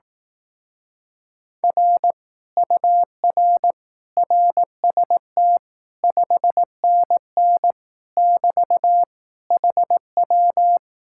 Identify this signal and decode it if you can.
Morse code, heard as a keyed tone, about 18 wpm, 700 Hz. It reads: R UR RST 5NN <BT> HW
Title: Speed-Racing Common QSO Phrases Encore - 15wpm